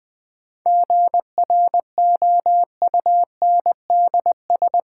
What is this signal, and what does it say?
Morse code, heard as a keyed tone, about 20 wpm, 700 Hz. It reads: GROUNDS
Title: Sentences from Top 1300 Words - New Word Drill Alt - 20wpm